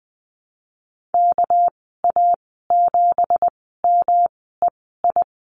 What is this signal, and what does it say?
Morse code, heard as a keyed tone, about 20 wpm, 700 Hz. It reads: K A 7 M E I
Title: 238 Lesson 38g - Letter q - Call Signs 20wpm